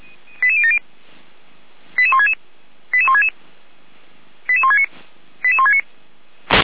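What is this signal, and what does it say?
Five-tone selective call (ZVEI), heard as quick runs of five tones, tones of about 70 ms. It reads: 88070 88166 88166 88169 88169
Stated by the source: Received on an SDR as a ZVEI five-tone call.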